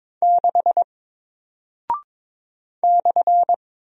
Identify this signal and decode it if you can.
Morse code, heard as a keyed tone, about 22 wpm, 700 Hz. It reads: 6 /
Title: Single Character (Letter:Number:Punctuation:Prosign) 22wpm